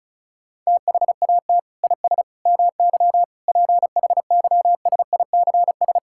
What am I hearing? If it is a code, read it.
Morse code, heard as a keyed tone, about 36 wpm, 700 Hz. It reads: THAT IS MY PHYSICS
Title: Sentences from Top 1000 Words - Review Part 5 - 35wpm